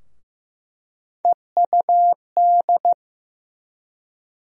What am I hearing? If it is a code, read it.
Morse code, heard as a keyed tone, about 15 wpm, 700 Hz. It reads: EUD
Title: USA Call Sign Postfixes - 3 Letters 15wpm